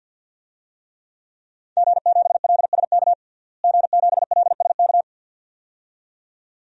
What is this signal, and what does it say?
Morse code, heard as a keyed tone, about 52 wpm, 700 Hz. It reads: G7LSK G7LSK
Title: DXCC-100 Call Signs - 5 Characters - Repeated 2x - 50wpm